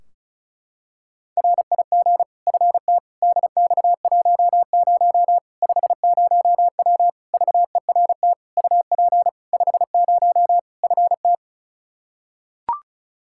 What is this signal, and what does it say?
Morse code, heard as a keyed tone, about 34 wpm, 700 Hz. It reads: RIG FT DX10 50W VERT UP 50 FT
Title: Practice QSO - Exchange 2 - Info - 35wpm